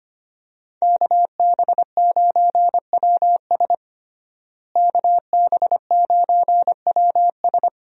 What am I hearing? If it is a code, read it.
Morse code, heard as a keyed tone, about 25 wpm, 700 Hz. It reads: KB9WS KB9WS
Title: USA Call Signs - 5 Characters - Repeated 2x - 25wpm